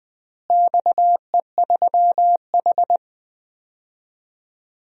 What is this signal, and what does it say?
Morse code, heard as a keyed tone, about 19 wpm, 700 Hz. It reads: XE3H
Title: DXCC-100 Call Signs - 4 Characters - Repeated 2x - 20wpm